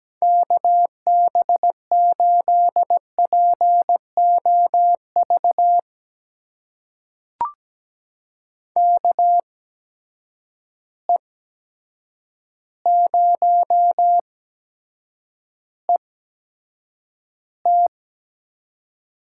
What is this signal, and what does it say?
Morse code, heard as a keyed tone, about 17 wpm, spaced as for 8 wpm, 700 Hz. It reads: KB8POV KE0ET
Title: USA Call Signs - 6 Characters - 8x Character Spacing - 17wpm